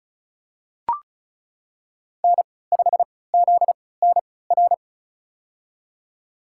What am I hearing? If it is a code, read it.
Morse code, heard as a keyed tone, about 34 wpm, spaced as for 20 wpm, 700 Hz. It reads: N5ZNR
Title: USA Call Signs - 5 Characters - 3x Character Spacing - 35wpm